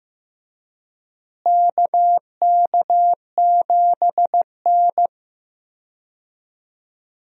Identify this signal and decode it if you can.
Morse code, heard as a keyed tone, about 15 wpm, 700 Hz. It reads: KK7N
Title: USA Call Signs - 4 Characters - Repeated 3x - 15wpm